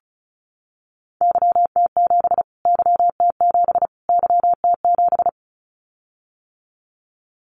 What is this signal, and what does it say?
Morse code, heard as a keyed tone, about 34 wpm, 700 Hz. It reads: YT7 YT7 YT7
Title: Callsign Prefixes for Top 25 DXCC - Repeated 3x - 35wpm